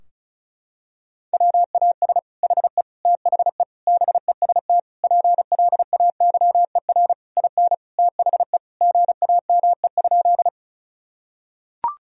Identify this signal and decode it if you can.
Morse code, heard as a keyed tone, about 36 wpm, 700 Hz. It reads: WAS HE THE BEST PLAYER IN THE GAME?
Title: Sentences from Top 1000 Words - Review Part 4 - 35wpm